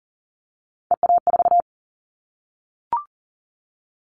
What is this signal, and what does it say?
Morse code, heard as a keyed tone, about 40 wpm, 700 Hz. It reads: EA4
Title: Callsign Prefixes for Top 25 DXCC - 40wpm